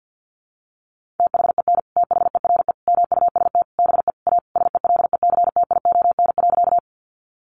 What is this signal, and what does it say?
Morse code, heard as a keyed tone, about 50 wpm, 700 Hz. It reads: THEN THERE MUST BE A SELECTION.
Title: Sentences from Top 1400 Words 50wpm